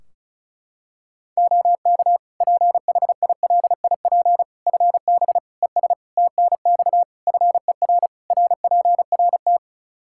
Morse code, heard as a keyed tone, about 34 wpm, 700 Hz. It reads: OK PHILIP FB ES TNX FER RPRT
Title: K1USN SST QSO Practice 35wpm